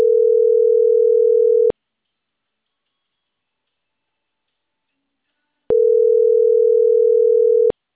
Ringback tone (North America), 440 + 480 Hz.